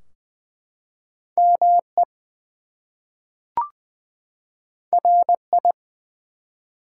Morse code, heard as a keyed tone, about 20 wpm, 700 Hz. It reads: ME RI
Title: US States and Canadian Provinces 20wpm